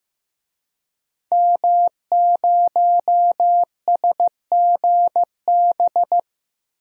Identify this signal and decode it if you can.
Morse code, heard as a keyed tone, about 15 wpm, 700 Hz. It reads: M0SGB
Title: DXCC-100 Call Signs - 5 Characters - Repeated 3x - 15wpm